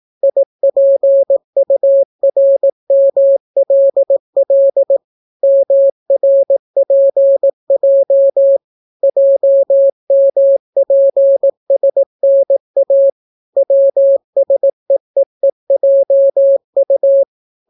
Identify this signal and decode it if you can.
Morse code, heard as a keyed tone, about 18 wpm, 550 Hz. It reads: IPURMLL MRPJ JMPSNA WSEEEJU